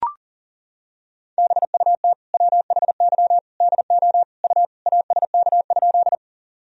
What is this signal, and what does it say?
Morse code, heard as a keyed tone, about 40 wpm, 700 Hz. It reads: BUT WHY DO U ASK?
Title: Sentences from Top 200 Words and Abbreviations 40wpm